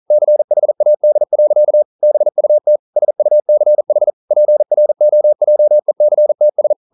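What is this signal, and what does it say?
Morse code, heard as a keyed tone, about 40 wpm, 600 Hz. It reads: CHAD. BUT SUCH PROJECTS
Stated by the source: Morse code at 40 wpm, practice text, CW med Gnister.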